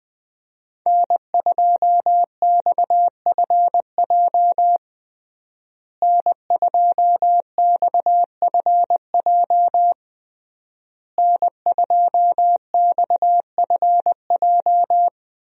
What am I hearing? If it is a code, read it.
Morse code, heard as a keyed tone, about 21 wpm, 700 Hz. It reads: N2XFJ N2XFJ N2XFJ
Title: USA Call Signs - 5 Characters - Repeated 3x - 20wpm